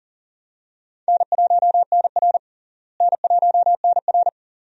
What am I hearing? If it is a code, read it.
Morse code, heard as a keyed tone, about 37 wpm, 700 Hz. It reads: N1NR N1NR
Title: USA Call Signs - 4 Characters - Repeated 2x - 40wpm